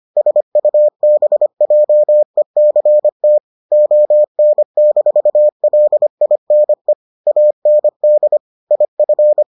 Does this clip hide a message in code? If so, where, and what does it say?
Yes: Morse code from 0.2 to 9.4 s, heard as a keyed tone, about 25 wpm, 600 Hz. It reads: SUBJECT ON-LINE AND IF